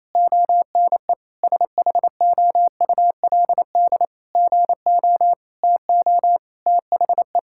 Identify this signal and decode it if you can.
Morse code, heard as a keyed tone, about 28 wpm, 700 Hz. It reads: ONE SHOULD GO TO THE
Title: Sentences from Top 1600 Words 28wpm